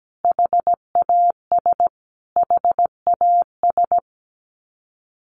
Morse code, heard as a keyed tone, about 17 wpm, 700 Hz. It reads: HAS HAS
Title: Top 200 Words Repeated 17wpm